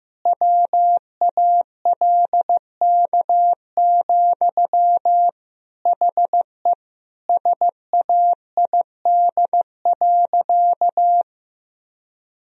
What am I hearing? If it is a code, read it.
Morse code, heard as a keyed tone, about 15 wpm, 700 Hz. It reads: WALK, HE SAID.
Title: Sentences from Top 1000 Words - Review Part 1 - 15wpm